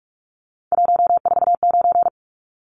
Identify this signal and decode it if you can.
Morse code, heard as a keyed tone, about 44 wpm, 700 Hz. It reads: 149
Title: Numbers - 3-Digit 45wpm